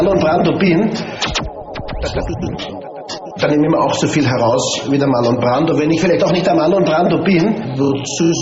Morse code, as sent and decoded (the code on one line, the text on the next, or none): none